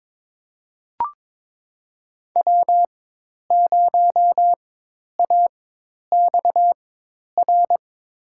Morse code, heard as a keyed tone, about 21 wpm, spaced as for 10 wpm, 700 Hz. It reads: W0AXR
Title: USA Call Signs - 4x Character Spacing - 22wpm